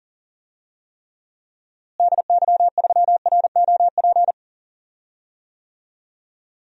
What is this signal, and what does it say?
Morse code, heard as a keyed tone, about 40 wpm, 700 Hz. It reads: DY3ROP
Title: DXCC-100 Call Signs - 6 Characters - Repeated 2x - 40wpm